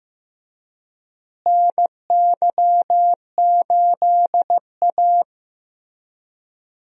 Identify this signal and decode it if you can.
Morse code, heard as a keyed tone, about 16 wpm, 700 Hz. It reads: NY8A
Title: USA Call Signs - 4 Characters - Repeated 3x - 15wpm